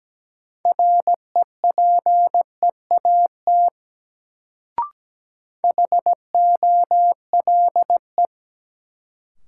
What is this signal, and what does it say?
Morse code, heard as a keyed tone, about 17 wpm, 700 Hz. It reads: REPEAT HOLE
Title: Sentences from Top 1600 Words - New Word Drill - 17wpm